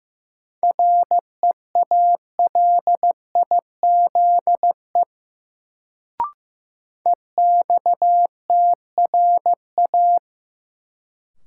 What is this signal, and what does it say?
Morse code, heard as a keyed tone, about 15 wpm, 700 Hz. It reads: REALIZE EXTRA
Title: Sentences from Top 1000 Words - New Word Drill - 15wpm